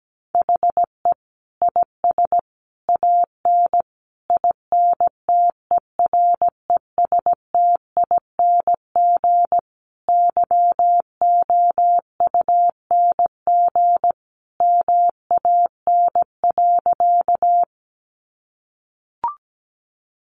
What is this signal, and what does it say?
Morse code, heard as a keyed tone, about 17 wpm, 700 Hz. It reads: HE IS AN INTERESTING YOUNG MAN.
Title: Sentences from Top 1000 Words 17wpm